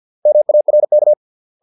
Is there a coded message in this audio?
Yes: Morse code from 0.3 to 1.1 s, heard as a keyed tone, about 50 wpm, 600 Hz. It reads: MARK